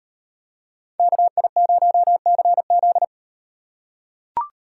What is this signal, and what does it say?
Morse code, heard as a keyed tone, about 38 wpm, 700 Hz. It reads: KI0CZ